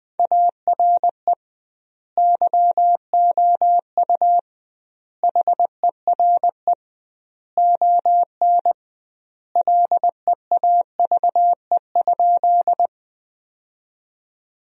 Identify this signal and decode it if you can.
Morse code, heard as a keyed tone, about 20 wpm, 700 Hz. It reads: ARE YOU HERE ON LEAVE?
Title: 2x Word-Spacing - Sentences from Top 400 Words - 20wpm